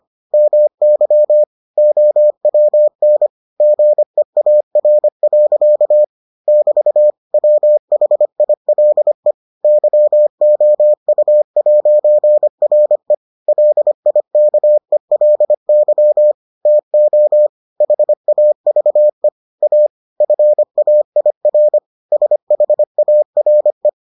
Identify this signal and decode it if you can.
Morse code, heard as a keyed tone, about 25 wpm, 600 Hz. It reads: MY OWN GEAR. <BT> WHILE YOU'RE LIKELY TO HAVE A FAIR SHARE